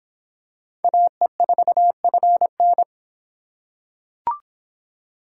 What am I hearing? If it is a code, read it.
Morse code, heard as a keyed tone, about 26 wpm, 700 Hz. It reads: AE4FN